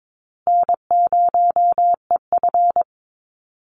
Morse code, heard as a keyed tone, about 22 wpm, 700 Hz. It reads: N0EF
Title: Call Signs Repeated 2x - 22wpm